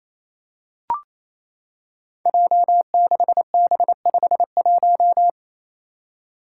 Morse code, heard as a keyed tone, about 28 wpm, 700 Hz. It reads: J6B51
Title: Five Challenging Letter-Number Combinations 28wpm